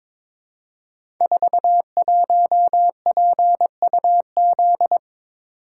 Morse code, heard as a keyed tone, about 22 wpm, 700 Hz. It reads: 41PUZ
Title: Code Groups - 5 Letter:Numbers - 22wpm